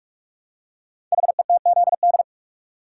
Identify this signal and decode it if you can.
Morse code, heard as a keyed tone, about 44 wpm, 700 Hz. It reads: HETZD
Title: Code Groups - 5 Letters - 45wpm